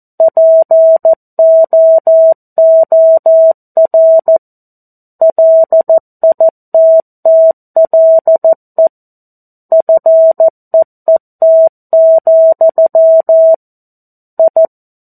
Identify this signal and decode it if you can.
Morse code, heard as a keyed tone, about 14 wpm, 650 Hz. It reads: POOR LITTLE FEET, I
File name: 02-Chapter-02 14 WPM